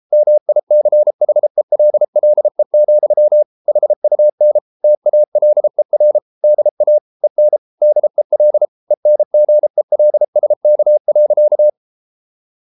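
Morse code, heard as a keyed tone, about 33 wpm, 600 Hz. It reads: MICHELLE, HUN TALER DA EN DEL ENGELSK.
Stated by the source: Morse code at 33 wpm, QRQ.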